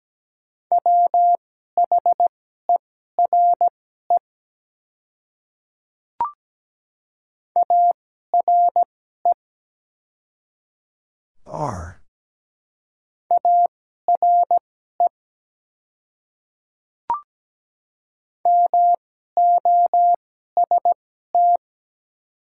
Morse code, heard as a keyed tone, about 17 wpm, spaced as for 12 wpm, 700 Hz. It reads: WHERE ARE ARE MOST
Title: 2x Character Spacing - Top 300 Words - 17wpm